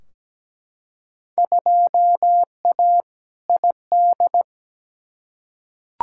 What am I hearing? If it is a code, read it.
Morse code, heard as a keyed tone, about 17 wpm, 700 Hz. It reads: 2A ID
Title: Field Day Exchanges - 17wpm